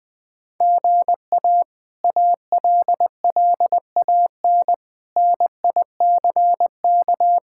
Morse code, heard as a keyed tone, about 20 wpm, 700 Hz. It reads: GA ALLAN NICK
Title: K1USN SST QSO Practice - 20wpm